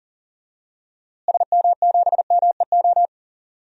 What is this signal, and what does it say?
Morse code, heard as a keyed tone, about 40 wpm, 700 Hz. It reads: SM7MEO
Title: DXCC-100 Call Signs - 6 Characters - Repeated 3x - 40wpm